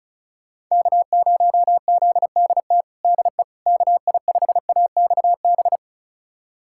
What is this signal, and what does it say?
Morse code, heard as a keyed tone, about 36 wpm, 700 Hz. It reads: K0ZDT DE KI5AXB
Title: USA Callsign DE Callsign 35wpm